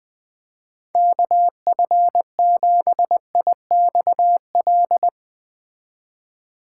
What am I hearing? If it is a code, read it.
Morse code, heard as a keyed tone, about 20 wpm, 700 Hz. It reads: KF7IXL